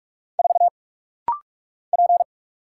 Morse code, heard as a keyed tone, about 46 wpm, 700 Hz. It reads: 4 P
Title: Single Letter-Number Alt - Rapid-Fire 45wpm